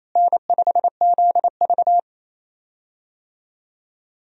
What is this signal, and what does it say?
Morse code, heard as a keyed tone, about 28 wpm, 700 Hz. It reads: N5ZV
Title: USA Call Signs - 4 Characters - Repeated 3x - 28wpm